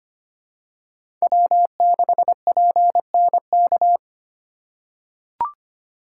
Morse code, heard as a keyed tone, about 25 wpm, 700 Hz. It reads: W6PNK
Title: USA Call Signs - 5 Characters - 2x Character Spacing - 25wpm